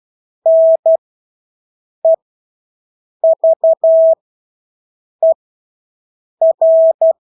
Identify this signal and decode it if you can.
Morse code, heard as a keyed tone, about 12 wpm, spaced as for 6 wpm, 650 Hz. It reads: NEVER
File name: Engelske ord 12-6 wpm